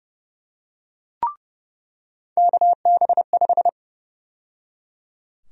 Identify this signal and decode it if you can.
Morse code, heard as a keyed tone, about 30 wpm, 700 Hz. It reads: KB5
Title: USA Call Sign Prefixes - 30wpm